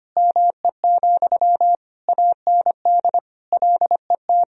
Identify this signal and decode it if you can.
Morse code, heard as a keyed tone, about 25 wpm, 700 Hz. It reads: ME, AND LET
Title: Sentences from Top 1000 Words - Review Part 2 - 25wpm